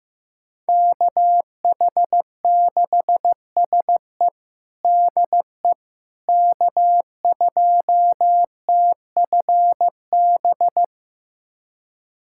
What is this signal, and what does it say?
Morse code, heard as a keyed tone, about 15 wpm, 700 Hz. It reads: KH6SE DE K2TFB
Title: USA Callsign DE Callsign - 5 Characters - 15wpm